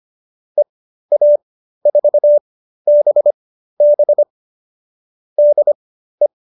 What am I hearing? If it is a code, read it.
Morse code, heard as a keyed tone, about 25 wpm, spaced as for 13 wpm, 590 Hz. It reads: EA4BB DE